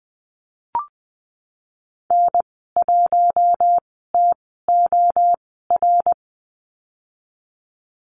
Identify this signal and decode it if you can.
Morse code, heard as a keyed tone, about 20 wpm, spaced as for 15 wpm, 700 Hz. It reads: N1TOR
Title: 055 Lesson 10g - Letter r - Call Signs 20wpm